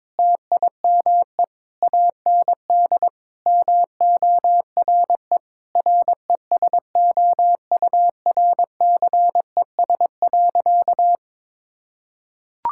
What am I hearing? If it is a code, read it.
Morse code, heard as a keyed tone, about 21 wpm, 700 Hz. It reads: TIME AND MORE RESOURCES.
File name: practice / 22wpm / Sentences with 2-Word Phrases - 22wpm